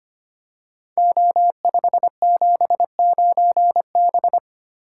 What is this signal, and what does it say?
Morse code, heard as a keyed tone, about 25 wpm, 700 Hz. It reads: O579B
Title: Code Groups - 5 Letter:Numbers - 25wpm